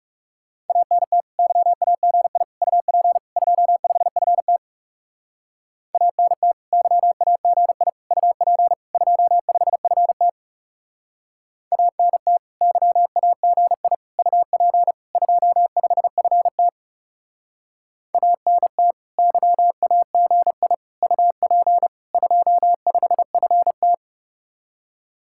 Morse code, heard as a keyed tone, about 36 wpm, 700 Hz. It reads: ANT YAGI UP 25FT ANT YAGI UP 25FT ANT YAGI UP 25FT ANT YAGI UP 25FT
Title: Speed-Racing Common QSO Phrases - 30wpm